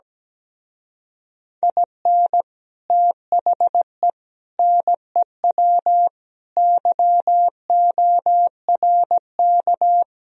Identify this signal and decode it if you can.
Morse code, heard as a keyed tone, about 17 wpm, 700 Hz. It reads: IN THE NEW YORK
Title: Speed-Racing - Sets of 5 Words 17wpm